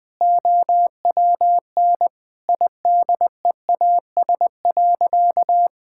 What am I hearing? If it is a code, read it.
Morse code, heard as a keyed tone, about 20 wpm, 700 Hz. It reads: OWN IDEAS.
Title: Sentences with 2-Word Phrases - 20wpm